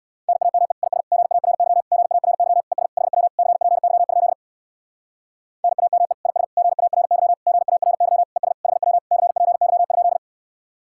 Morse code, heard as a keyed tone, about 64 wpm, 700 Hz. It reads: NAME IS GARY GARY ES HW CPY? NAME IS GARY GARY ES HW CPY?